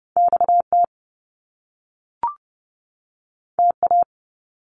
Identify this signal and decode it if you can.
Morse code, heard as a keyed tone, about 30 wpm, 700 Hz. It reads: XT TA
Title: Top Two Letter Combinations in Words 30wpm